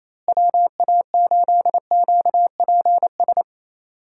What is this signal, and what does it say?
Morse code, heard as a keyed tone, about 28 wpm, 700 Hz. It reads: WA8QPS
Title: USA Call Signs - 6 Characters - Repeated 3x - 28wpm